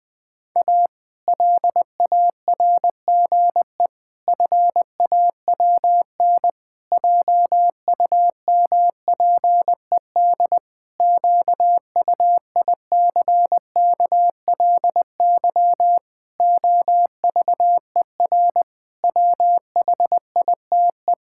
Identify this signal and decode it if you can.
Morse code, heard as a keyed tone, about 21 wpm, 700 Hz. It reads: A LARGE FAWN JUMPED QUICKLY OVER WHITE
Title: Pangram Challenge v2 - 20wpm